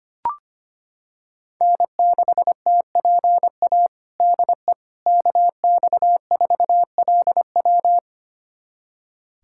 Morse code, heard as a keyed tone, about 25 wpm, 700 Hz. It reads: N6TPA DE KX4LW
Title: USA Callsign DE Callsign - 5 Characters - 25wpm